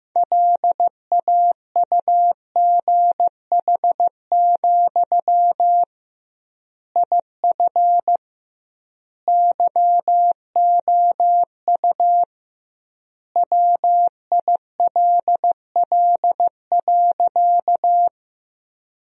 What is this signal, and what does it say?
Morse code, heard as a keyed tone, about 15 wpm, 700 Hz. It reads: LAUGH, IF YOU WILL.
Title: 2x Word-Spacing - Sentences from Top 500 Words - 15wpm